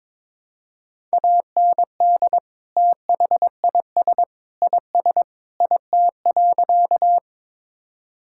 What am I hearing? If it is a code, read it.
Morse code, heard as a keyed tone, about 22 wpm, 700 Hz. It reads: AND THIS IS IT.
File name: practice / 22wpm / Sentences with 3-Word Phrases - 22wpm